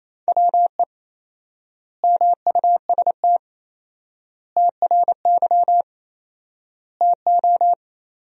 Morse code, heard as a keyed tone, about 28 wpm, 700 Hz. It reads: WE MUST TRY TO
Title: 4x Word-Spacing - Sentences from Top 400 Words - 28wpm